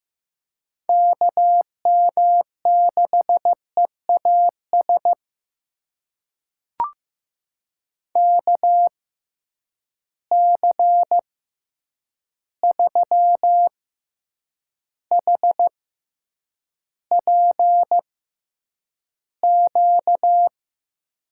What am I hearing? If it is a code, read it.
Morse code, heard as a keyed tone, about 15 wpm, spaced as for 8 wpm, 700 Hz. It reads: KM6EAS KC3HPQ
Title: USA Call Signs - 6 Characters - 6x Character Spacing - 15wpm